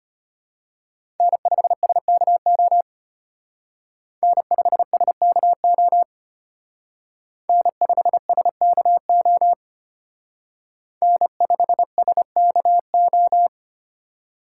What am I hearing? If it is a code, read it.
Morse code, heard as a keyed tone, about 31 wpm, 700 Hz. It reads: N5SKO N5SKO N5SKO N5SKO
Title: Speed Racing - USA 1x3 Call Signs 25wpm